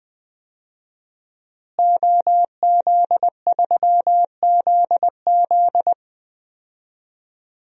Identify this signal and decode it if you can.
Morse code, heard as a keyed tone, about 20 wpm, 700 Hz. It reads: OZ3ZZ
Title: DXCC-100 Call Signs - 5 Characters - Repeated 3x - 20wpm